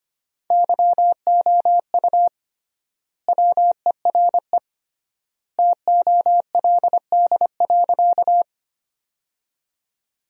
Morse code, heard as a keyed tone, about 25 wpm, 700 Hz. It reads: YOU WERE TOLD.
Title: 3x Word-Spacing - Sentences from Top 400 Words - 25wpm